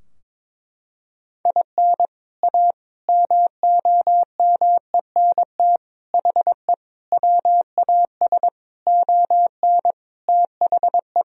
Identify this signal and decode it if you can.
Morse code, heard as a keyed tone, about 22 wpm, 700 Hz. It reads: IN A MOMENT HE WAS ON THE
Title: Sentences from Top 1000 Words - Review Part 2 - 22wpm